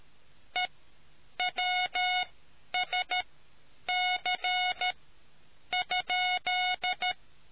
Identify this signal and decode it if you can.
Morse code, heard as a keyed tone, about 13 wpm, spaced as for 8 wpm, 730 Hz. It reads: EWSC?